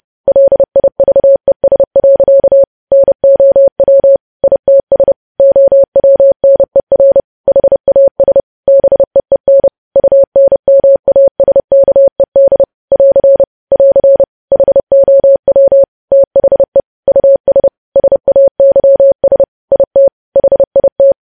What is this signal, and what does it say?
Morse code, heard as a keyed tone, about 30 wpm, 550 Hz. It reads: LIVES. NOW ITS OWNER HAS BEEN UNMASKED <AR> <AR> HOW THE US SAYS IT HIT
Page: news headlines in morse code at 30 wpm